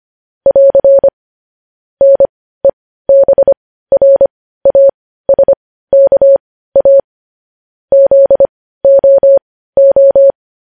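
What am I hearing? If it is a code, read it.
Morse code, heard as a keyed tone, about 25 wpm, spaced as for 15 wpm, 550 Hz. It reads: <AR> NEBRASKA ZOO
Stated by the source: Morse news headlines practice at 15 wpm.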